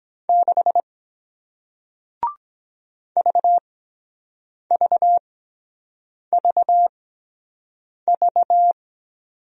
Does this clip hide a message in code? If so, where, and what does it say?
Yes: Morse code from 0.3 to 8.7 s, heard as a keyed tone, about 24 wpm, spaced as for 7 wpm, 700 Hz. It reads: 6 VVVV